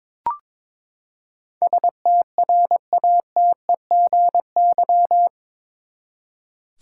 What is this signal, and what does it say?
Morse code, heard as a keyed tone, about 22 wpm, 700 Hz. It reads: STRATEGY
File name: practice / 22wpm / Eight Letter Words - 22wpm